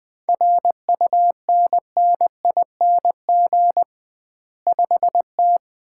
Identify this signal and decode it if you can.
Morse code, heard as a keyed tone, about 21 wpm, 700 Hz. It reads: RUNNING 5T
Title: Common QSO Phrases - 2x Word-Spacing - 20wpm